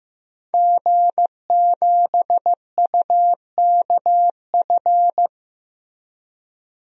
Morse code, heard as a keyed tone, about 15 wpm, 700 Hz. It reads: G7UKF
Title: DXCC-100 Call Signs - 5 Characters - Repeated 2x - 15wpm